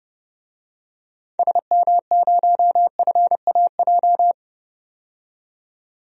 Morse code, heard as a keyed tone, about 30 wpm, 700 Hz. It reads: SM0FAJ
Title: DXCC-100 Call Signs - 6 Characters - 30wpm